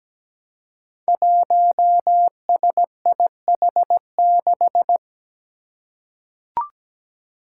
Morse code, heard as a keyed tone, about 17 wpm, 700 Hz. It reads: 1SIH6
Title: Five Challenging Letter-Number Combinations 17wpm